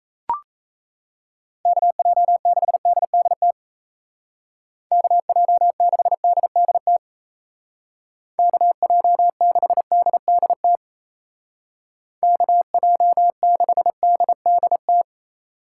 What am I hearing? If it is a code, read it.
Morse code, heard as a keyed tone, about 34 wpm, 700 Hz. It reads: KJ6DDT KJ6DDT KJ6DDT KJ6DDT
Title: Speed Racing - USA 2x3 Call Signs 28wpm